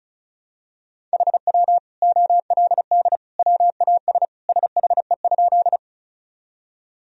Morse code, heard as a keyed tone, about 36 wpm, 690 Hz. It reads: HW OLD WAS SHE?
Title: Sentences from Top 200 Words and Abbreviations 35wpm